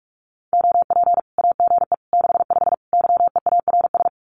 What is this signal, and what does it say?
Morse code, heard as a keyed tone, about 44 wpm, 700 Hz. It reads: OP AGE 65 YEARS
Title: Common QSO Phrases 45wpm